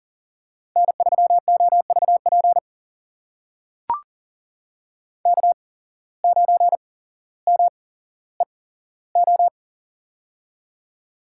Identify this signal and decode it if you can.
Morse code, heard as a keyed tone, about 38 wpm, spaced as for 17 wpm, 700 Hz. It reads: N3OVP K9MEO
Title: USA Call Signs - 5 Characters - 8x Character Spacing - 40wpm